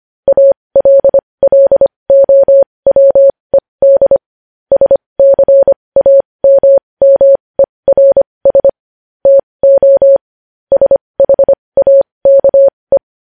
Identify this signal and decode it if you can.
Morse code, heard as a keyed tone, about 25 wpm, spaced as for 20 wpm, 550 Hz. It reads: ALLOWED SCAMMERS TO SHAKE
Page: news-headlines-in-morse-code-at-20-wpm